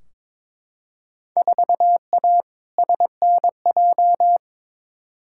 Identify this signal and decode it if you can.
Morse code, heard as a keyed tone, about 21 wpm, 700 Hz. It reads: 4A SNJ
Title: Field Day Exchanges Encore - 22wpm